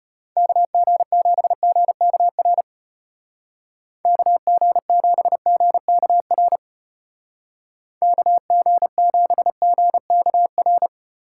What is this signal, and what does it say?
Morse code, heard as a keyed tone, about 32 wpm, 700 Hz. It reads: KG7GKR KG7GKR KG7GKR